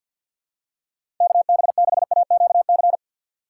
Morse code, heard as a keyed tone, about 47 wpm, 700 Hz. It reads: KB6AQC
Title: USA Call Signs - 6 Characters - 6x Character Spacing - 50wpm